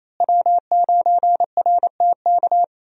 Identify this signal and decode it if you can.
Morse code, heard as a keyed tone, about 28 wpm, 700 Hz. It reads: W9RTK
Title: USA Call Signs - 5 Characters - Repeated 2x - 28wpm